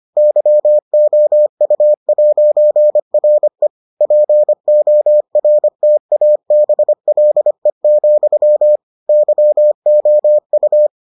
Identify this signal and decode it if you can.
Morse code, heard as a keyed tone, about 25 wpm, 600 Hz. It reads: YOU'RE PORTABLE, YOU